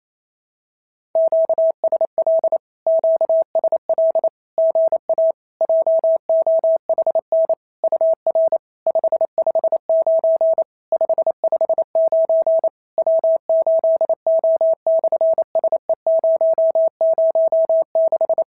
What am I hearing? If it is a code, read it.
Morse code, heard as a keyed tone, about 28 wpm, 650 Hz. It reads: QSL QSL GA JOHN UR 559 559 W8O/SE006
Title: SOTA QSO Practice 28wpm